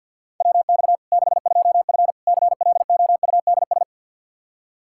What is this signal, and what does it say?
Morse code, heard as a keyed tone, about 52 wpm, 700 Hz. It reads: WX 62F CLOUDS
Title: K1USN SST QSO Practice 50wpm